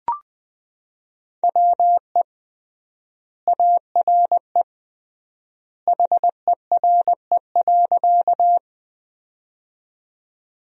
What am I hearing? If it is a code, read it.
Morse code, heard as a keyed tone, about 20 wpm, 700 Hz. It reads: WE ARE HERE.
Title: 3x Word-Spacing - Sentences from Top 200 Words - 20wpm